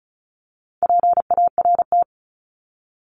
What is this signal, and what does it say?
Morse code, heard as a keyed tone, about 34 wpm, 700 Hz. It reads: PART
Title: ES:S Suffix - Top 100 Words with Base Words 35wpm